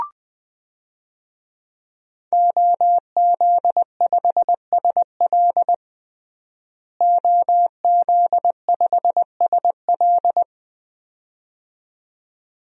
Morse code, heard as a keyed tone, about 20 wpm, 700 Hz. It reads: OZ5SL OZ5SL
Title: DXCC-100 Call Signs - 5 Characters - Repeated 2x - 20wpm